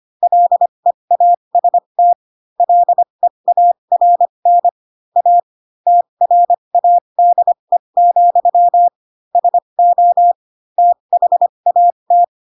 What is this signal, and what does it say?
Morse code, heard as a keyed tone, about 25 wpm, spaced as for 22 wpm, 700 Hz. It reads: LEAST LEARN A TRADE, SO THAT
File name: Pinocchio / Pinocchio ch4 18 25 ik1hge